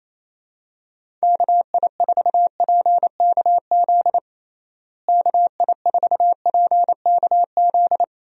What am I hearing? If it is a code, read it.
Morse code, heard as a keyed tone, about 28 wpm, 700 Hz. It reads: KI4PKZ KI4PKZ
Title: USA Call Signs - 6 Characters - Repeated 3x - 28wpm